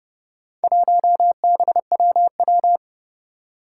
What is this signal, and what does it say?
Morse code, heard as a keyed tone, about 30 wpm, 700 Hz. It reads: 1BWW
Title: Four Challenging Letter-Number Combinations 30wpm